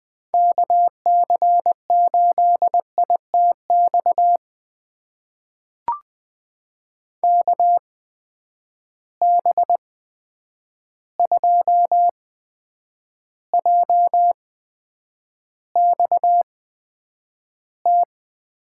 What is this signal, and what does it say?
Morse code, heard as a keyed tone, about 20 wpm, spaced as for 9 wpm, 700 Hz. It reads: KC8ITX KB2JXT
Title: USA Call Signs - 6 Characters - 8x Character Spacing - 20wpm